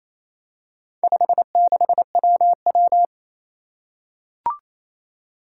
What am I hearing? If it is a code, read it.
Morse code, heard as a keyed tone, about 27 wpm, 700 Hz. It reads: 56WW
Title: Four Challenging Letter-Number Combinations 28wpm